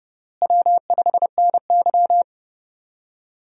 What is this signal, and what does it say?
Morse code, heard as a keyed tone, about 30 wpm, 700 Hz. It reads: W5NY
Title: USA Call Signs - 4 Characters - Repeated 3x - 30wpm